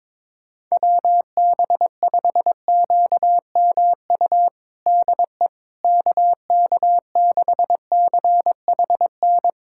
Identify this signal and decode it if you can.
Morse code, heard as a keyed tone, about 22 wpm, 700 Hz. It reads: WB5QMU DE KK6CHN
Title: USA Callsign DE Callsign 22wpm